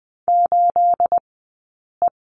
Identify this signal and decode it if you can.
Morse code, heard as a keyed tone, about 20 wpm, spaced as for 8 wpm, 700 Hz. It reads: 8E
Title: 211 Lesson 34d - Number 8 - Two Characters 20wpm